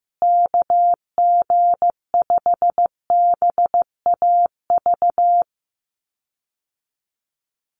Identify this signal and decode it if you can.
Morse code, heard as a keyed tone, about 15 wpm, 700 Hz. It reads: KG5BAV
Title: USA Call Signs Repeated 2x 15wpm